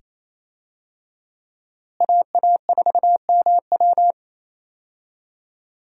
Morse code, heard as a keyed tone, about 28 wpm, 700 Hz. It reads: AA4MW
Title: USA Call Signs - 5 Characters - Repeated 2x - 28wpm